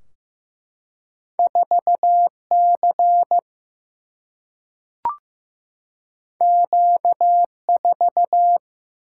Morse code, heard as a keyed tone, about 15 wpm, 700 Hz. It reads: 4C Q4
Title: Single-Letter Single-Number 15wpm